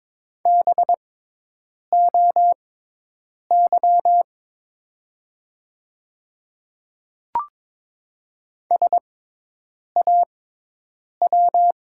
Morse code, heard as a keyed tone, about 22 wpm, spaced as for 8 wpm, 700 Hz. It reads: BOY SAW
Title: 6x Character Spacing - Top 300 Words - 22wpm